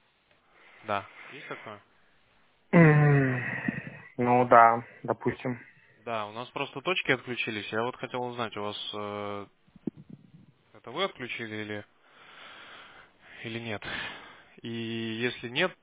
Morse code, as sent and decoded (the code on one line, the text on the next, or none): none